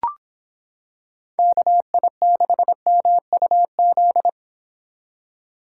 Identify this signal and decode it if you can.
Morse code, heard as a keyed tone, about 26 wpm, 700 Hz. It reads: KI6MUZ